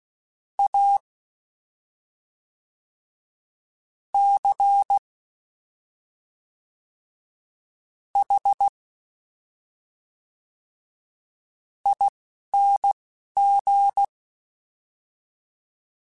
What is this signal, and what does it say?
Morse code, heard as a keyed tone, about 16 wpm, spaced as for 4 wpm, 800 Hz. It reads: ACHING
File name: Special INGs / Special INGs-mp3s / Fast ing @10